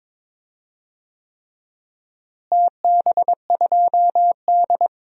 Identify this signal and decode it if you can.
Morse code, heard as a keyed tone, about 21 wpm, 700 Hz. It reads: TB2D
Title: DXCC-100 Call Signs - 4 Characters - Repeated 3x - 22wpm